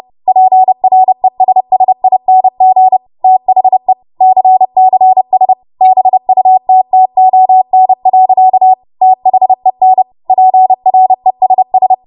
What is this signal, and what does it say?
Morse code, heard as a keyed tone, about 30 wpm, 750 Hz. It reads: PRESSING THE CCS BUTTON. THEN PRESS
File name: Morse / 30WPM